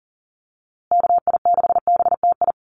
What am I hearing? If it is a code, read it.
Morse code, heard as a keyed tone, about 40 wpm, 700 Hz. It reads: KI6BTI
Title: USA Call Signs Repeated 2x 40wpm